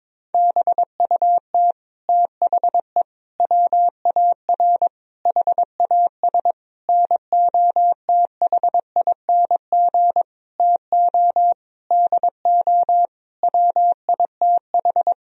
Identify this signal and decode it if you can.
Morse code, heard as a keyed tone, about 22 wpm, 700 Hz. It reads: BUT THE WAR HAS NOTHING TO DO WITH
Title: Sentences from Top 1000 Words - Review Part 2 - 22wpm